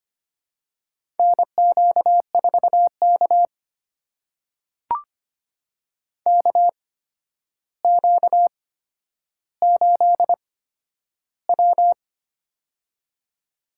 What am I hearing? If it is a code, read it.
Morse code, heard as a keyed tone, about 25 wpm, spaced as for 11 wpm, 700 Hz. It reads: NQ4K KQ8W
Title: USA Call Signs - 4 Characters - 8x Character Spacing - 25wpm